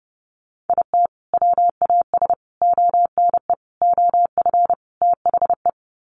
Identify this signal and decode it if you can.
Morse code, heard as a keyed tone, about 30 wpm, 700 Hz. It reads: IT WAS ONE OF THE
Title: Sets of 5 Words 30wpm